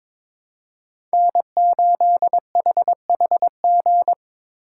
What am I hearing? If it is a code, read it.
Morse code, heard as a keyed tone, about 22 wpm, 700 Hz. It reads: N8HHG
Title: K1USN SST QSO Practice - 22wpm